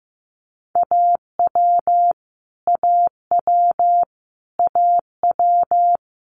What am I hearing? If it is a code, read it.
Morse code, heard as a keyed tone, about 15 wpm, 700 Hz. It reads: AW AW AW
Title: Multi-letter Phonemes Repeated 3x 15wpm